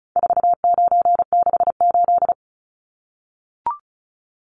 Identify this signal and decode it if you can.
Morse code, heard as a keyed tone, about 34 wpm, 700 Hz. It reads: 4968
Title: Numbers - 4-Digit 35wpm